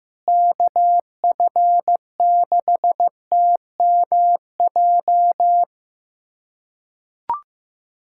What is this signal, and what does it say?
Morse code, heard as a keyed tone, about 15 wpm, 700 Hz. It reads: KF6TMJ
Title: USA Callsign DE Callsign - 6 Characters - 15wpm